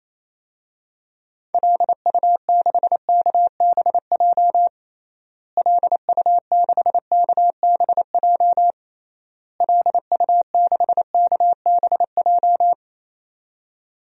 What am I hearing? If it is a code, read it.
Morse code, heard as a keyed tone, about 27 wpm, 700 Hz. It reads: LU6KBJ LU6KBJ LU6KBJ
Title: DXCC-100 Call Signs - 6 Characters - Repeated 3x - 28wpm